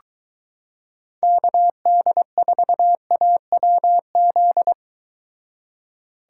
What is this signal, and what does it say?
Morse code, heard as a keyed tone, about 23 wpm, 700 Hz. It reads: KD4AWZ